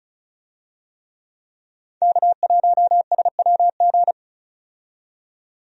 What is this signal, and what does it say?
Morse code, heard as a keyed tone, about 36 wpm, 700 Hz. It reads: K1SWG
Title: USA Call Signs - 5 Characters - 35wpm